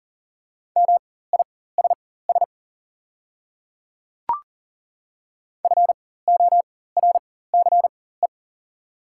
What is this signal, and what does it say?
Morse code, heard as a keyed tone, about 38 wpm, spaced as for 18 wpm, 690 Hz. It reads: MISS FORCE
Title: 4x Character Spacing - Top 500 Words - 40wpm